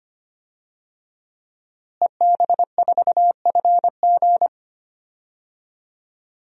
Morse code, heard as a keyed tone, about 25 wpm, 700 Hz. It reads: EB4FG
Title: DXCC-100 Call Signs - 5 Characters - Repeated 3x - 25wpm